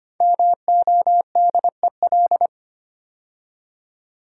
Morse code, heard as a keyed tone, about 25 wpm, 700 Hz. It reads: MODEL
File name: 5 Letter Words - First 3 Letters Given - 25wpm